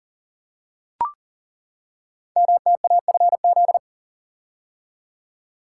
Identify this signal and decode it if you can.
Morse code, heard as a keyed tone, about 38 wpm, 690 Hz. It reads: MTAFZ
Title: Code Groups - 5 Letters - 40wpm